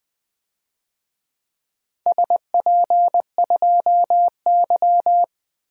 Morse code, heard as a keyed tone, about 20 wpm, 700 Hz. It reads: SP2Y
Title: DXCC-100 Call Signs - 4 Characters - 20wpm